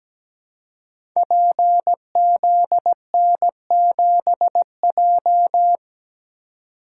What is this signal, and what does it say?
Morse code, heard as a keyed tone, about 17 wpm, 700 Hz. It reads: PZN7J
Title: Code Groups - 5 Letter:Numbers - 17wpm